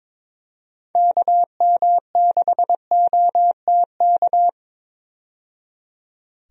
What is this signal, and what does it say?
Morse code, heard as a keyed tone, about 22 wpm, 700 Hz. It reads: KM6OTK
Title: USA Call Signs - 6 Characters - 22wpm